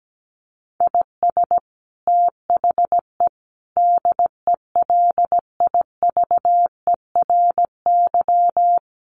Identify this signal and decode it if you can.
Morse code, heard as a keyed tone, about 17 wpm, 700 Hz. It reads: IS THE DELIVERY
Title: Sentences from Top 1500 Words 17wpm